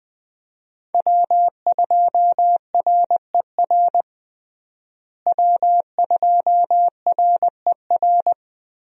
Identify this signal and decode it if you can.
Morse code, heard as a keyed tone, about 20 wpm, 700 Hz. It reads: W2RER W2RER
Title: USA Call Signs - 5 Characters - Repeated 2x - 20wpm